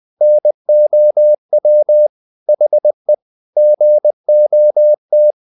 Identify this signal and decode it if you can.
Morse code, heard as a keyed tone, about 20 wpm, 600 Hz. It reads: NOW HE GOT
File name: the old man and the sea part 2 20wpm